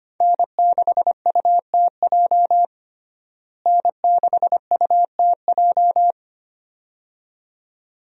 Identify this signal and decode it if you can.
Morse code, heard as a keyed tone, about 25 wpm, 700 Hz. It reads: N6UTJ N6UTJ
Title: USA Call Signs - 5 Characters - Repeated 2x - 25wpm